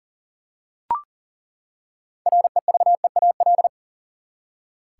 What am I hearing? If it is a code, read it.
Morse code, heard as a keyed tone, about 40 wpm, 700 Hz. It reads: REVEAL
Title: Six Letter Words - 40wpm